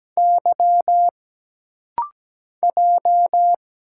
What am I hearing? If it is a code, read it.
Morse code, heard as a keyed tone, about 17 wpm, 700 Hz. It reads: Y J
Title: Single Letters - Rapid-Fire 17wpm